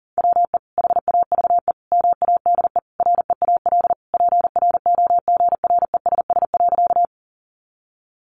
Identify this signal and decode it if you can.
Morse code, heard as a keyed tone, about 37 wpm, 700 Hz. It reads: WE HAVE MADE REAL PROGRESS.
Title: Sentences from Top 1400 Words 40wpm